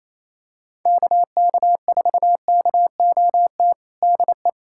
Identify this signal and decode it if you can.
Morse code, heard as a keyed tone, about 27 wpm, 700 Hz. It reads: KK4KOT DE
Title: USA Callsign DE Callsign 28wpm